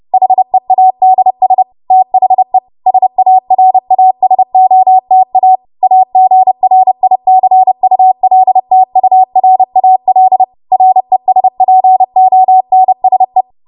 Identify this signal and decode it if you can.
Morse code, heard as a keyed tone, about 30 wpm, 750 Hz. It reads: HEADS THE SARASOTA AGRICULTURAL RESPONSE
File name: Morse / 30WPM